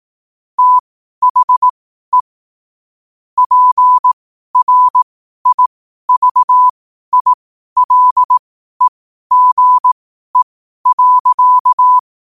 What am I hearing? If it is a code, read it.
Morse code, heard as a keyed tone, about 18 wpm, spaced as for 12 wpm, 1000 Hz. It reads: THE PRIVILEGE.
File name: qotdfiles / 12wpm